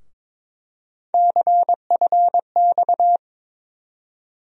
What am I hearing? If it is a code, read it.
Morse code, heard as a keyed tone, about 22 wpm, 700 Hz. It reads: CFX
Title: USA Call Sign Postfixes - 3 Letters 22wpm